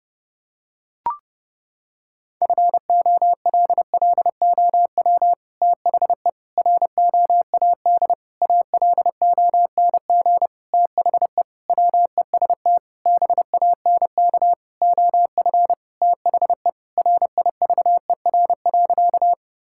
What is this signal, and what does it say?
Morse code, heard as a keyed tone, about 30 wpm, 700 Hz. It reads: FOLLOW THE ROAD ALONG THE WEST BANK OF THE RIVER.
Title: Sentences from Top 1000 Words - Review Part 6 - 30wpm